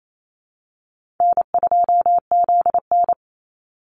Morse code, heard as a keyed tone, about 28 wpm, 700 Hz. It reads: N2ZN
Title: Easy Call Signs - 28wpm